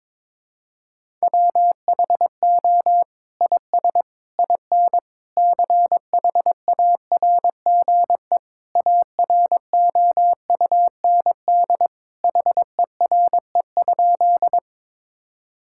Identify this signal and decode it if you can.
Morse code, heard as a keyed tone, about 22 wpm, 700 Hz. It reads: WHO IS IN CHARGE AROUND HERE?
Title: Sentences from Top 1000 Words - Review Part 6 - 22wpm